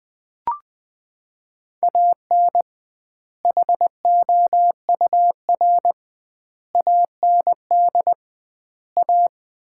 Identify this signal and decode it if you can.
Morse code, heard as a keyed tone, about 20 wpm, 700 Hz. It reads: AN HOUR AND A
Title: 2x Word-Spacing - Sets of 5 Words - 20wpm